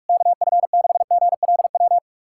Morse code, heard as a keyed tone, about 48 wpm, 700 Hz. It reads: KF6GLW
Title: SOTA QSO Practice 45wpm